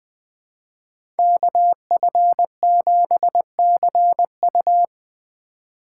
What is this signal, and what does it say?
Morse code, heard as a keyed tone, about 21 wpm, 700 Hz. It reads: KF7CU
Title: USA Call Signs - 2x Character Spacing - 20wpm